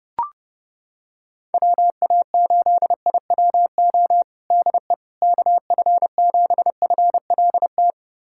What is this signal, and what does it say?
Morse code, heard as a keyed tone, about 30 wpm, 700 Hz. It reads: WA8IWO DE KF7FLT
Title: USA Callsign DE Callsign 30wpm